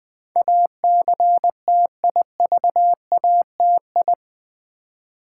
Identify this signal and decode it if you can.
Morse code, heard as a keyed tone, about 20 wpm, 700 Hz. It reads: ACTIVATI
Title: Build A Word - 4 Letters At A Time - 10-Letter Words - 20wpm